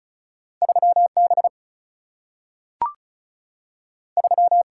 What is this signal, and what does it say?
Morse code, heard as a keyed tone, about 36 wpm, 700 Hz. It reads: 3B 3
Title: Single-Letter Single-Number 35wpm